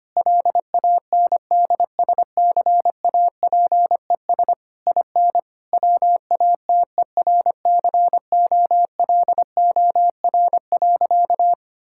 Morse code, heard as a keyed tone, about 25 wpm, 700 Hz. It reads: LANDSCAPES IN WATERCOLOR.
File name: practice / 25wpm / Difficult Sentences with Hints - 25wpm